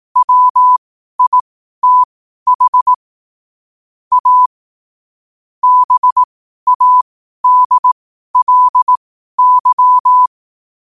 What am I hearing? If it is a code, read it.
Morse code, heard as a keyed tone, about 18 wpm, spaced as for 12 wpm, 1000 Hz. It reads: WITH A BADLY
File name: qotdfiles / 12wpm